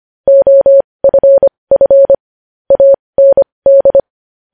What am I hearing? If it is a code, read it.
Morse code, heard as a keyed tone, about 25 wpm, spaced as for 20 wpm, 550 Hz. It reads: OFF AND
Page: News Headlines in Morse Code at 20 WPM